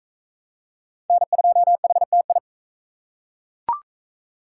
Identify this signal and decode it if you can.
Morse code, heard as a keyed tone, about 40 wpm, 700 Hz. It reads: N2HTI